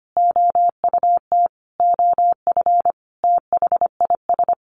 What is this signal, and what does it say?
Morse code, heard as a keyed tone, about 25 wpm, 700 Hz. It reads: OUT OF THIS
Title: Sentences from Top 1200 Words 25wpm